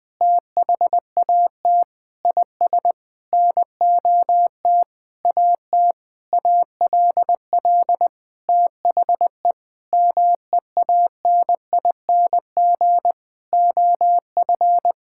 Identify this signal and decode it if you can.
Morse code, heard as a keyed tone, about 21 wpm, 700 Hz. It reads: THAT IS NOT AT ALL THE MEANING OF
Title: Sentences from Top 1000 Words - Review Part 6 - 20wpm